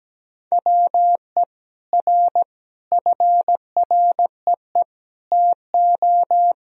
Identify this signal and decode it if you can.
Morse code, heard as a keyed tone, about 17 wpm, 700 Hz. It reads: WE R FREE TO
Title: Sentences from Top 500 Words and Abbreviations 17wpm